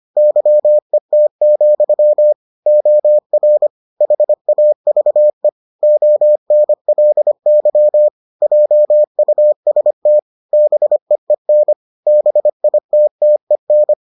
Morse code, heard as a keyed tone, about 25 wpm, 600 Hz. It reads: YET, OR HAVE ONLY JUST BEEN BITTEN